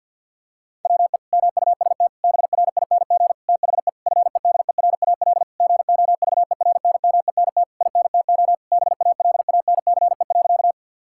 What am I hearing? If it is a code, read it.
Morse code, heard as a keyed tone, about 50 wpm, 700 Hz. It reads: WE MUST BRING THE FEDERAL GOVERNMENT INTO BALANCE.
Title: Sentences from Top 1000 Words - Review Part 6 - 50wpm